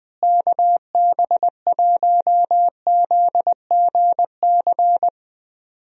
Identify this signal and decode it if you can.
Morse code, heard as a keyed tone, about 20 wpm, 700 Hz. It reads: KB1ZGC
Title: USA Call Signs - 6 Characters - Repeated 2x - 20wpm